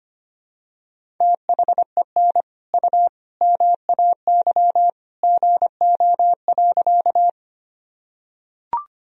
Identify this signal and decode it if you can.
Morse code, heard as a keyed tone, about 25 wpm, 700 Hz. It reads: THEN U MAY GO.
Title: Sentences from Top 100 Words and Abbreviations 25wpm